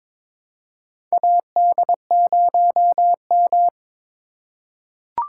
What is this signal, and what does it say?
Morse code, heard as a keyed tone, about 23 wpm, 700 Hz. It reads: AD0M
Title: USA Call Signs - 2x Character Spacing - 22wpm